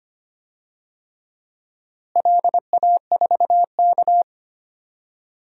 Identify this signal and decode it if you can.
Morse code, heard as a keyed tone, about 25 wpm, 700 Hz. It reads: LA4K
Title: DXCC-100 Call Signs - 4 Characters - Repeated 3x - 25wpm